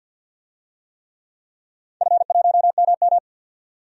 Morse code, heard as a keyed tone, about 51 wpm, 700 Hz. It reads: F1MM
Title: DXCC-100 Call Signs - 4 Characters - 50wpm